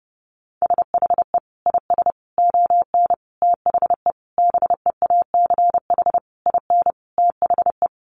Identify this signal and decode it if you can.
Morse code, heard as a keyed tone, about 30 wpm, 700 Hz. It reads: SHE IS ON THE BEACH IN THE
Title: Sentences from Top 1500 Words 30wpm